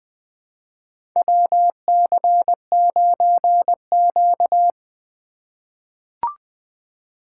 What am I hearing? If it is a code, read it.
Morse code, heard as a keyed tone, about 20 wpm, 700 Hz. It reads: WC9Q
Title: USA Call Signs - 4 Characters - 2x Character Spacing - 20wpm